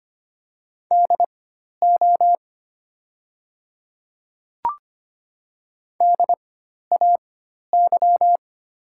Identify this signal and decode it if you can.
Morse code, heard as a keyed tone, about 25 wpm, spaced as for 12 wpm, 690 Hz. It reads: DO DAY
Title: 4x Character Spacing - Top 200 Words - 25wpm